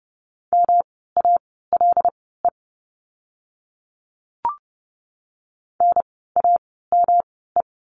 Morse code, heard as a keyed tone, about 30 wpm, spaced as for 17 wpm, 700 Hz. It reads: MALE NAME
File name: course / 30:10wpm / 108 Lesson 18f - Letter m - Words 30wpm